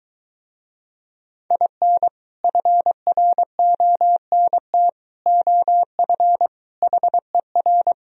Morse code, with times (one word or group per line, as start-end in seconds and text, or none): 1.5-2.1 IN
2.4-4.9 FRONT
5.3-6.5 OF
6.8-7.9 HER